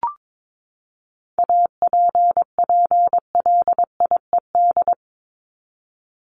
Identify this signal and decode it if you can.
Morse code, heard as a keyed tone, about 22 wpm, 700 Hz. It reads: APPLIED
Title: ED Suffix - Top 100 Words with Base Words 22wpm